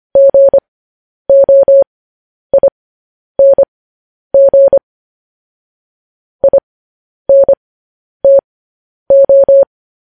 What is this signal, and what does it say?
Morse code, heard as a keyed tone, about 25 wpm, spaced as for 10 wpm, 550 Hz. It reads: GOING INTO
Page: news headlines in morse code at 10 wpm